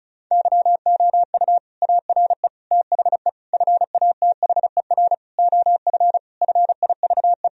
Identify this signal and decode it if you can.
Morse code, heard as a keyed tone, about 34 wpm, 700 Hz. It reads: YOU ARE THE FATHER OF FIVE
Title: Sentences from Top 1000 Words - Review Part 2 - 35wpm